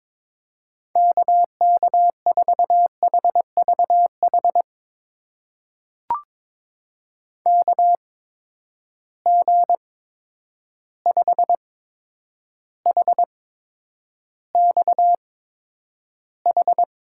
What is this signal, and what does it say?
Morse code, heard as a keyed tone, about 22 wpm, spaced as for 9 wpm, 700 Hz. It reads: KK4HVH KG5HXH
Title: USA Call Signs - 6 Characters - 8x Character Spacing - 22wpm